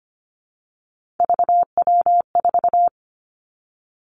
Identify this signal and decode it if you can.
Morse code, heard as a keyed tone, about 25 wpm, 700 Hz. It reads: VW4
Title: Three Challenging Letter-Number Combinations 25wpm